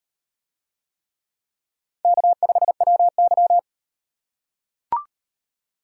Morse code, heard as a keyed tone, about 36 wpm, 700 Hz. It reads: K5WY